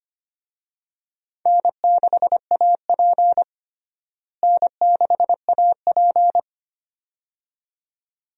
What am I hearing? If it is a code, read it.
Morse code, heard as a keyed tone, about 25 wpm, 700 Hz. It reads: N6AP N6AP
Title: USA Call Signs - 4 Characters - Repeated 2x - 25wpm